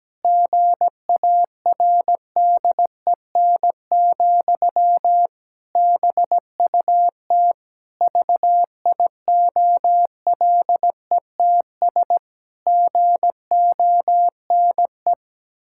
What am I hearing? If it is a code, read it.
Morse code, heard as a keyed tone, about 17 wpm, 700 Hz. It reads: GARDEN, BUT VIOLETS GONE
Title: Pangram Challenge v2 - 17wpm